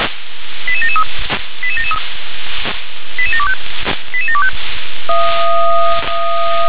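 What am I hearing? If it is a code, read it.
Five-tone selective call (ZVEI), heard as quick runs of five tones, tones of about 70 ms. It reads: 88773 88773 88725 88725
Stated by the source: ZVEI five-tone selective call 88713 received on an SDR.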